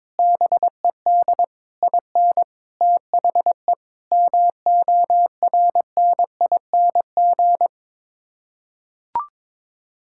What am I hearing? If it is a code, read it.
Morse code, heard as a keyed tone, about 22 wpm, 700 Hz. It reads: BED IN THE MORNING
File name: practice / 22wpm / Sets of 8 Words - 22wpm